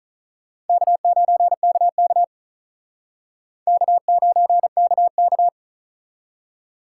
Morse code, heard as a keyed tone, about 43 wpm, 700 Hz. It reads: K9KK K9KK
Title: Speed Racing - USA 1x2 Call Signs 35wpm